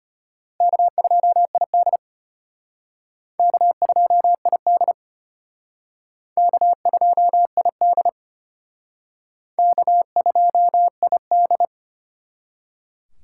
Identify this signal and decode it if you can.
Morse code, heard as a keyed tone, about 33 wpm, 700 Hz. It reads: K2ID K2ID K2ID K2ID